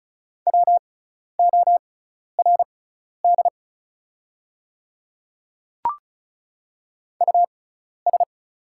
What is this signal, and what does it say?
Morse code, heard as a keyed tone, about 36 wpm, spaced as for 12 wpm, 690 Hz. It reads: WORD US
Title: 6x Character Spacing - Top 300 Words - 35wpm